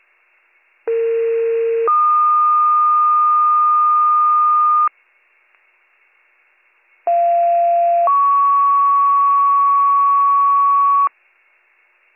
Two-tone page, heard as a one-second tone, then a three-second tone, about 460 then 1220 Hz, about 690 then 1120 Hz. It is heard from 0.9 to 11.1 s.